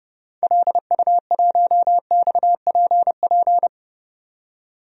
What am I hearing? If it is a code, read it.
Morse code, heard as a keyed tone, about 30 wpm, 700 Hz. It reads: LU1XPP
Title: DXCC-100 Call Signs - 6 Characters - 30wpm